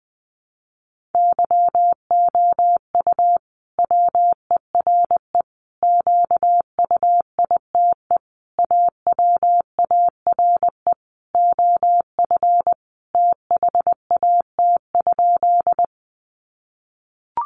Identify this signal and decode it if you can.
Morse code, heard as a keyed tone, about 20 wpm, 700 Hz. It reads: YOU WERE QUITE AWARE OF THAT?
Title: Sentences from Top 1200 Words 20wpm